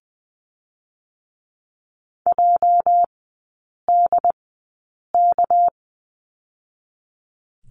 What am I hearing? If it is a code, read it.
Morse code, heard as a keyed tone, about 20 wpm, 700 Hz. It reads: J D K
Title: 218 Lesson 35e - Number 0 - Three Characters 20wpm